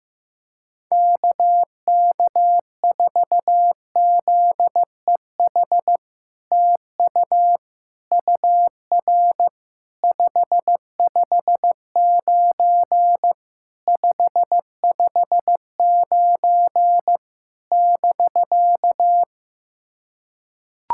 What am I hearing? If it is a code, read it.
Morse code, heard as a keyed tone, about 15 wpm, 700 Hz. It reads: KK4ZEH TU UR 559 559 <BK>
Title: SOTA QSO Practice 15wpm